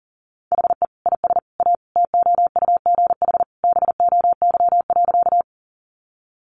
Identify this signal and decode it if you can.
Morse code, heard as a keyed tone, about 40 wpm, 700 Hz. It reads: HE IS A TOUGH BOY.
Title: Sentences from Top 1200 Words 40wpm